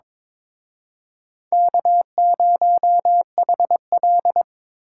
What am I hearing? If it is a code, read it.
Morse code, heard as a keyed tone, about 26 wpm, 700 Hz. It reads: K0HL
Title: Speed Racing - USA 1x2 Call Signs 22wpm